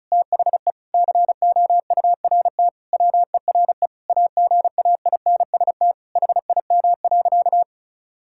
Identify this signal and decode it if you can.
Morse code, heard as a keyed tone, about 36 wpm, 700 Hz. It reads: THE COURT WERE AGAINST HIM.
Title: Sentences from Top 1600 Words 35wpm